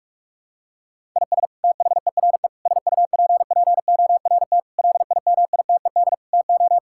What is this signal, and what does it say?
Morse code, heard as a keyed tone, about 46 wpm, 700 Hz. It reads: IS THERE SUPPORT LIMITED TO
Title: Sentences from Top 1000 Words - Review Part 6 - 45wpm